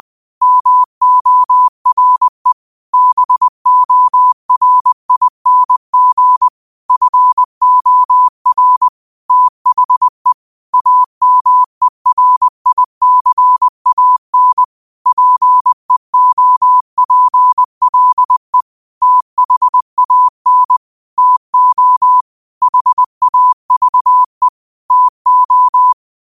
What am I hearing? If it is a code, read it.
Morse code, heard as a keyed tone, about 20 wpm, 1000 Hz. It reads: MORE BORING FOR THE AMERICAN PEOPLE THAN TO HAVE TO